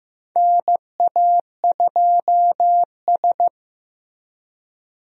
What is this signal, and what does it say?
Morse code, heard as a keyed tone, about 15 wpm, 700 Hz. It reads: NA2S
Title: USA Call Signs - 4 Characters - Repeated 3x - 15wpm